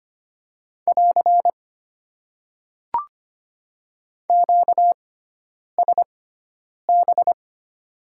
Morse code, heard as a keyed tone, about 25 wpm, spaced as for 9 wpm, 700 Hz. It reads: <AR> QSB
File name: Top 90 QSO Elements - 6x Character Spacing - 25wpm